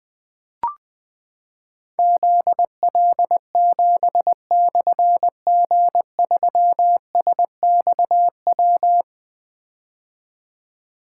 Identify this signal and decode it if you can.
Morse code, heard as a keyed tone, about 20 wpm, 700 Hz. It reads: ZL7/G3SXW
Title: Difficult Call Signs 20wpm